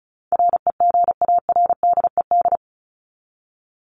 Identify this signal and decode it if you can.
Morse code, heard as a keyed tone, about 34 wpm, 700 Hz. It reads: REGARDED
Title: ED Suffix - Top 100 Words with Base Words 35wpm